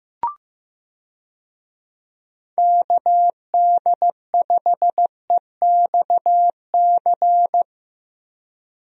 Morse code, heard as a keyed tone, about 15 wpm, 700 Hz. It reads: KD5EXC
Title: USA Call Signs - 6 Characters - 15wpm